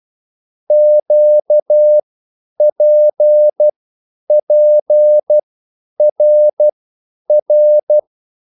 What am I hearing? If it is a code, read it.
Morse code, heard as a keyed tone, about 12 wpm, spaced as for 9 wpm, 600 Hz. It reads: QPPRR
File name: QSL / CW lesson